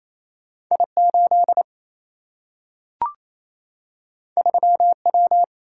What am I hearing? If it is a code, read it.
Morse code, heard as a keyed tone, about 28 wpm, 700 Hz. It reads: I8 3W
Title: Single-Letter Single-Number 28wpm